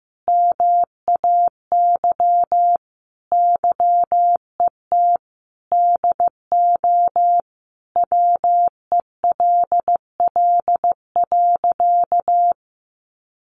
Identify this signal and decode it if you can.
Morse code, heard as a keyed tone, about 15 wpm, 700 Hz. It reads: MAY YET DO WELL.
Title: Sentences from Top 1400 Words 15wpm